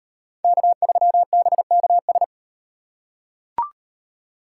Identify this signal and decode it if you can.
Morse code, heard as a keyed tone, about 36 wpm, 700 Hz. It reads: K3BKS